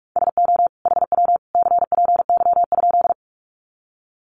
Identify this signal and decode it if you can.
Morse code, heard as a keyed tone, about 45 wpm, 700 Hz. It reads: SO HW CPY?
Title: Common QSO Phrases 45wpm